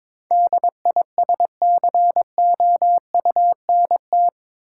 Morse code, heard as a keyed tone, about 21 wpm, 700 Hz. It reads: DISCOUNT
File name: practice / 22wpm / Eight Letter Words - 22wpm